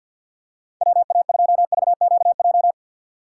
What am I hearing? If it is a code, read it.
Morse code, heard as a keyed tone, about 47 wpm, 700 Hz. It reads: WA2VQJ
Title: USA Call Signs - 6 Characters - 4x Character Spacing - 50wpm